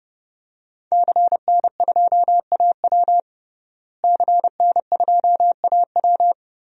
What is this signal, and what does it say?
Morse code, heard as a keyed tone, about 30 wpm, 700 Hz. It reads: CN2AW CN2AW
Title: DXCC-100 Call Signs - 5 Characters - Repeated 2x - 30wpm